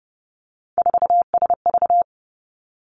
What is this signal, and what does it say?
Morse code, heard as a keyed tone, about 30 wpm, 700 Hz. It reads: 4SV
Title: Three Challenging Letter-Number Combinations 30wpm